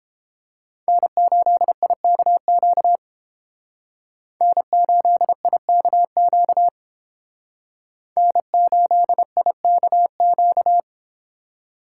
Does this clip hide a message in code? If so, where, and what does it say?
Yes: Morse code from 0.9 to 10.8 s, heard as a keyed tone, about 28 wpm, 700 Hz. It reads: N8IKQ N8IKQ N8IKQ